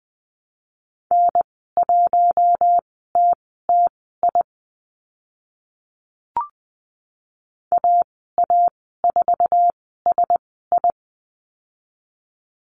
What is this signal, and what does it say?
Morse code, heard as a keyed tone, about 20 wpm, spaced as for 14 wpm, 700 Hz. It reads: N1TTI AA4SI
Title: 048 Lesson 9g - Number 4 - Call Signs 20wpm